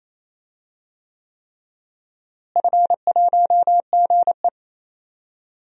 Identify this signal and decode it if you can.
Morse code, heard as a keyed tone, about 27 wpm, 700 Hz. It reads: F1GE
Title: DXCC-100 Call Signs - 4 Characters - Repeated 3x - 28wpm